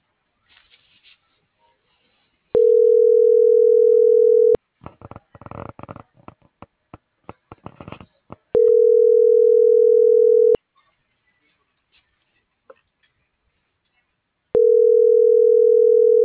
Ringback tone (North America), 440 + 480 Hz.